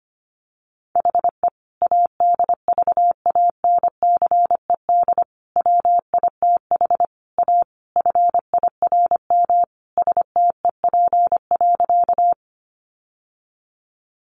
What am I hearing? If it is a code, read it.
Morse code, heard as a keyed tone, about 25 wpm, 700 Hz. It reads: HE ADVANCED WITH A FIRM STEP.
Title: Sentences from Top 1400 Words 25wpm